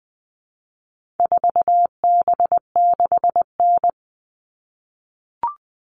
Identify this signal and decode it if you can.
Morse code, heard as a keyed tone, about 20 wpm, 700 Hz. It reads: 4B6N
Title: All Cut Numbers - 4-Digit 20wpm